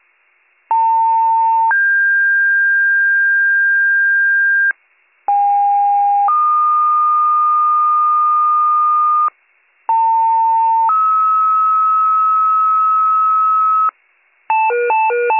Two-tone page, heard as a one-second tone, then a three-second tone, about 900 then 1600 Hz, about 810 then 1220 Hz, about 900 then 1290 Hz. It is heard from 0.7 to 13.9 s.